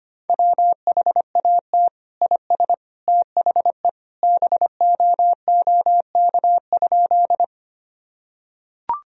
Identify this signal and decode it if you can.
Morse code, heard as a keyed tone, about 25 wpm, 700 Hz. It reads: WHAT IS THE BOOK?
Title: Sentences from Top 1000 Words - Review Part 1 - 25wpm